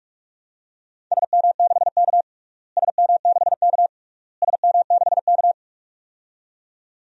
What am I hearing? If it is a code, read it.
Morse code, heard as a keyed tone, about 41 wpm, 700 Hz. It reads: SM6K SM6K SM6K
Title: DXCC-100 Call Signs - 4 Characters - Repeated 3x - 45wpm